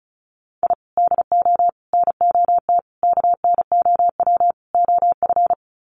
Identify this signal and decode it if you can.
Morse code, heard as a keyed tone, about 34 wpm, 700 Hz. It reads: I DO NOT KNOW OF
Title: Sets of 5 Words Encore2 35wpm